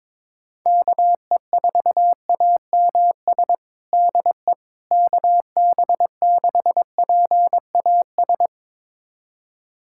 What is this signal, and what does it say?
Morse code, heard as a keyed tone, about 22 wpm, 700 Hz. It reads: KE4AMS DE KB6PAS
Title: USA Callsign DE Callsign 22wpm